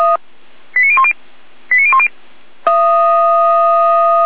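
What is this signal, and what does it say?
Five-tone selective call (ZVEI), heard as quick runs of five tones, tones of about 70 ms. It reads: 89010 89010
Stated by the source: ZVEI five-tone selective call received on an SDR.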